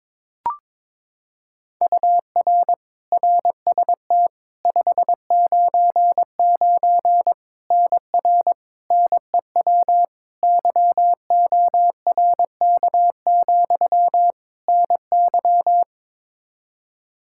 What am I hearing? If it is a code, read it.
Morse code, heard as a keyed tone, about 22 wpm, 700 Hz. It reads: UR RST 599 NR NEW YORK, NY
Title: K1USN SST QSO Practice 22wpm